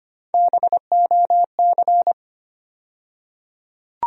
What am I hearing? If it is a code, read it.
Morse code, heard as a keyed tone, about 25 wpm, 700 Hz. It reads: BOC
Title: USA Call Sign Postfixes - 3 Letters 25wpm